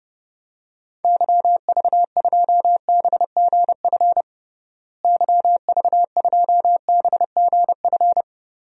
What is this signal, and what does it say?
Morse code, heard as a keyed tone, about 30 wpm, 700 Hz. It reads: YV2BGF YV2BGF
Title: DXCC-100 Call Signs - 6 Characters - Repeated 2x - 30wpm